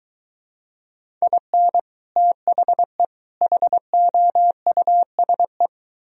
Morse code, heard as a keyed tone, about 23 wpm, 700 Hz. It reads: IN THE HOUSE